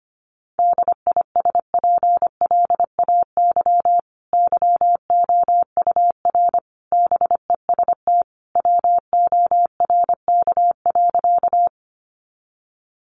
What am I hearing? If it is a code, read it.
Morse code, heard as a keyed tone, about 25 wpm, 700 Hz. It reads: DISPLAY YOUR BEST WORK.
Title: Sentences from Top 1200 Words 25wpm